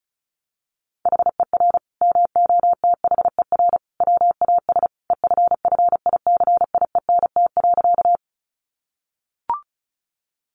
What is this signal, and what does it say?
Morse code, heard as a keyed tone, about 34 wpm, 700 Hz. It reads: HER MOTHER WAS EFFICIENT.
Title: Sentences from Top 1300 Words 35wpm